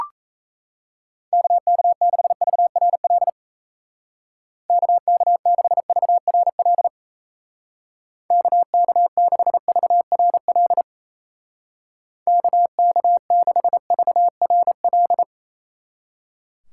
Morse code, heard as a keyed tone, about 35 wpm, 700 Hz. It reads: KK6VRL KK6VRL KK6VRL KK6VRL